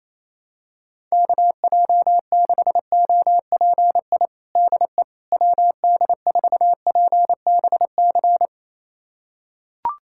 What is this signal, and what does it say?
Morse code, heard as a keyed tone, about 27 wpm, 700 Hz. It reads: KJ6OPI DE WD4PBC
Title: USA Callsign DE Callsign - 6 Characters - 28wpm